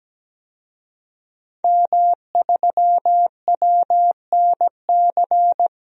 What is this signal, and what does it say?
Morse code, heard as a keyed tone, about 17 wpm, 700 Hz. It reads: M3WNC
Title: DXCC-100 Call Signs - 5 Characters - Repeated 2x - 17wpm